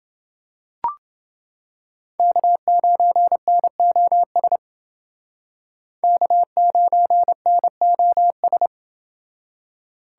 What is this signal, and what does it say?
Morse code, heard as a keyed tone, about 28 wpm, 700 Hz. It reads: K9NOS K9NOS